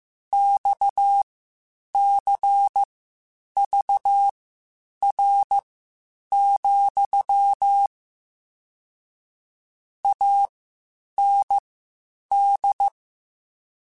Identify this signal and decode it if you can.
Morse code, heard as a keyed tone, about 15 wpm, spaced as for 8 wpm, 800 Hz. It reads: XCVR, AND
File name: QSOs / QSOs-mp3s / WD0FBT 1@13